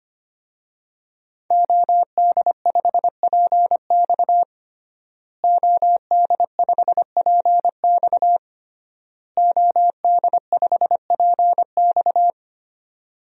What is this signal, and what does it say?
Morse code, heard as a keyed tone, about 25 wpm, 700 Hz. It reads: OD5PX OD5PX OD5PX
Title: DXCC-100 Call Signs - 5 Characters - Repeated 3x - 25wpm